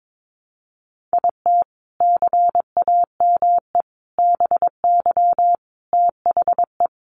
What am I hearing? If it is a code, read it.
Morse code, heard as a keyed tone, about 22 wpm, 700 Hz. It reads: IT CAME BY THE
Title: Sentences from Top 1000 Words 22wpm